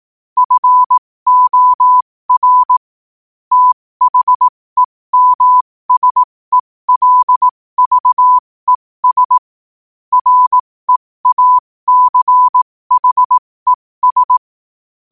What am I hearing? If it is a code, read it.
Morse code, heard as a keyed tone, about 18 wpm, spaced as for 15 wpm, 1000 Hz. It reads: FOR THEMSELVES REACHES
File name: qotdfiles / 15wpm